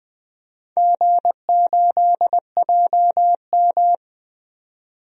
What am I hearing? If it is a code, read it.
Morse code, heard as a keyed tone, about 20 wpm, 700 Hz. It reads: G8JM
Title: DXCC-100 Call Signs - 4 Characters - 20wpm